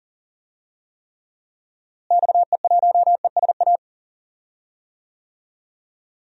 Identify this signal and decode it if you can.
Morse code, heard as a keyed tone, about 40 wpm, 700 Hz. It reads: XE1ESA
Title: DXCC-100 Call Signs - 6 Characters - Repeated 2x - 40wpm